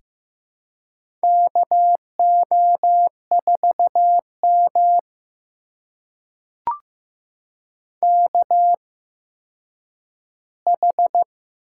Morse code, heard as a keyed tone, about 15 wpm, spaced as for 6 wpm, 700 Hz. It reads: KO4MKH
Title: USA Call Signs - 8x Character Spacing - 15wpm